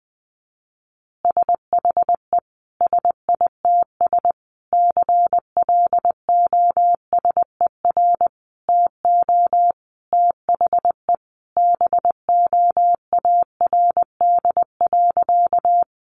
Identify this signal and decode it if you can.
Morse code, heard as a keyed tone, about 20 wpm, 700 Hz. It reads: SHE SITS CLOSER TO THE BOARD.
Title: Additional Easy Phrases 20wpm